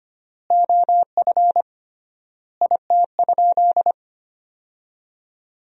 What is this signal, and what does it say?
Morse code, heard as a keyed tone, about 25 wpm, 700 Hz. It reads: OF IT?
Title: 3x Word-Spacing - Sentences from Top 100 Words - 25wpm